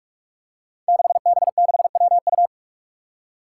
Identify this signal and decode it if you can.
Morse code, heard as a keyed tone, about 46 wpm, 700 Hz. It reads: 6B6WU
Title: Five Challenging Letter-Number Combinations 45wpm